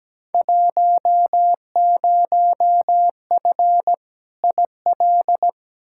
Morse code, heard as a keyed tone, about 17 wpm, 700 Hz. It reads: 10F IL
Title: Field Day Exchanges Encore - 17wpm